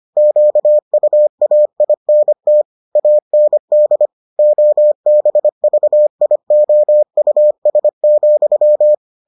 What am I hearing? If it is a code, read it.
Morse code, heard as a keyed tone, about 25 wpm, 600 Hz. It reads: QUAINT AND OBVIOUS,